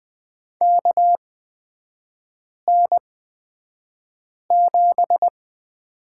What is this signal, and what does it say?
Morse code, heard as a keyed tone, about 20 wpm, spaced as for 5 wpm, 700 Hz. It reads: KN7
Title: 224 Lesson 36e - BT - Three Characters 20wpm